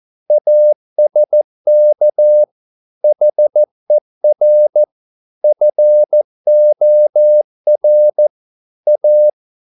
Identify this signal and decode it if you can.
Morse code, heard as a keyed tone, about 14 wpm, 600 Hz. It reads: ASK HER FOR A